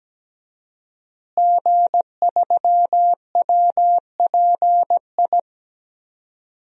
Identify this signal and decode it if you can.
Morse code, heard as a keyed tone, about 17 wpm, 700 Hz. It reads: G3WPI
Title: DXCC-100 Call Signs - 5 Characters - Repeated 2x - 17wpm